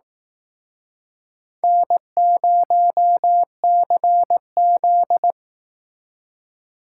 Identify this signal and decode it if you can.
Morse code, heard as a keyed tone, about 18 wpm, 700 Hz. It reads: N0CZ
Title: Speed Racing - USA 1x2 Call Signs 15wpm